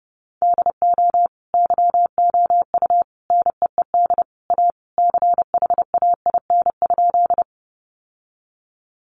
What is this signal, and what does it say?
Morse code, heard as a keyed tone, about 30 wpm, 700 Hz. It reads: DO YOU NEED A CHAIN?
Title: Sentences from Top 1500 Words 30wpm